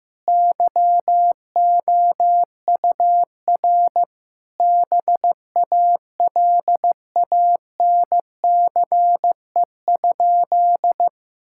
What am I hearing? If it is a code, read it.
Morse code, heard as a keyed tone, about 15 wpm, 700 Hz. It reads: YOUR BALANCE?
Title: Sentences from Top 1000 Words - Review Part 6 - 15wpm